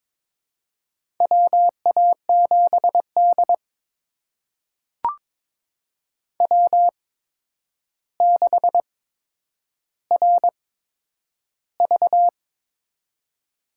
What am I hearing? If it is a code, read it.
Morse code, heard as a keyed tone, about 23 wpm, spaced as for 10 wpm, 700 Hz. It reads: WA7D W6RV
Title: USA Call Signs - 4 Characters - 8x Character Spacing - 22wpm